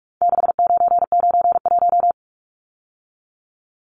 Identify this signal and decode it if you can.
Morse code, heard as a keyed tone, about 44 wpm, 700 Hz. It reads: 6991
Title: Numbers - 4-Digit 45wpm